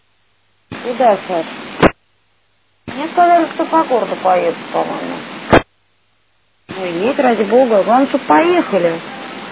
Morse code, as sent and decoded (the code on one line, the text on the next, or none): none